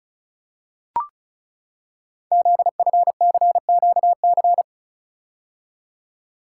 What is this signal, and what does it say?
Morse code, heard as a keyed tone, about 36 wpm, 700 Hz. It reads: ZFCQC
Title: Code Groups - 5 Letters - 35wpm